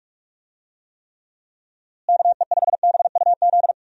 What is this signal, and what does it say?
Morse code, heard as a keyed tone, about 46 wpm, 700 Hz. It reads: KE5BUZ
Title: USA Call Signs - 6 Characters - Repeated 2x - 45wpm